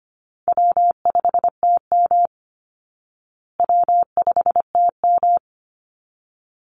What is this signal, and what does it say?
Morse code, heard as a keyed tone, about 25 wpm, 700 Hz. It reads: W5TM W5TM
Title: Easy Call Signs Repeated 2x - 25wpm